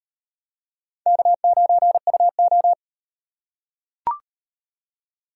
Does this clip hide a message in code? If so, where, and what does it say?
Yes: Morse code from 1.1 to 2.7 s, heard as a keyed tone, about 38 wpm, 700 Hz. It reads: K9UO